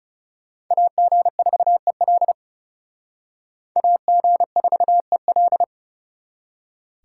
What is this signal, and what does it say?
Morse code, heard as a keyed tone, about 37 wpm, 700 Hz. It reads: AG4EL AG4EL
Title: Speed Racing - USA 2x2 Call Signs 30wpm